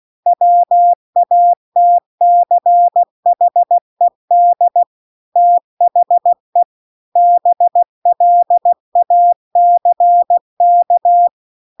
Morse code, heard as a keyed tone, about 16 wpm, 700 Hz. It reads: WATCHED THE BLACK